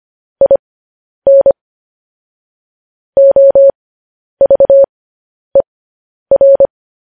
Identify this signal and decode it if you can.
Morse code, heard as a keyed tone, about 25 wpm, spaced as for 10 wpm, 550 Hz. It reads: IN OVER
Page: news-headlines-in-morse-code-at-10-wpm